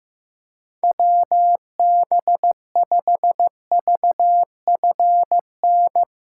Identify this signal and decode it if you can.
Morse code, heard as a keyed tone, about 16 wpm, 700 Hz. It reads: WB5VFN
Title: USA Callsign DE Callsign - 6 Characters - 15wpm